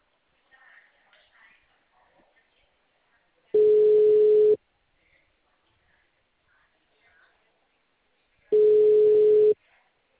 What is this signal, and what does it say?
Ringback tone (Europe, 425 Hz), with speech.